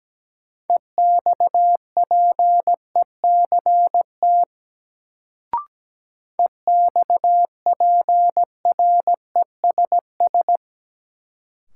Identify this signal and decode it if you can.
Morse code, heard as a keyed tone, about 17 wpm, 700 Hz. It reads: EXPECT EXPRESS
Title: Sentences from Top 1100 Words - New Word Drill Alt - 17wpm